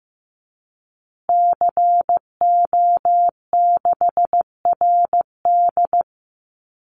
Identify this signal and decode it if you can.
Morse code, heard as a keyed tone, about 15 wpm, 700 Hz. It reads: CO6RD
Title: Call Signs Repeated 3x - 15wpm